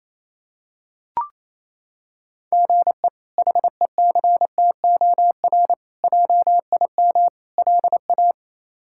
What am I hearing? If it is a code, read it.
Morse code, heard as a keyed tone, about 28 wpm, 700 Hz. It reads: GE HECTOR JIM LA
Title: K1USN SST QSO Practice - 28wpm